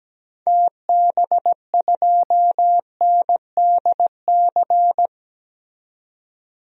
Morse code, heard as a keyed tone, about 17 wpm, 700 Hz. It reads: TB2NDC
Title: DXCC-100 Call Signs - 6 Characters - 17wpm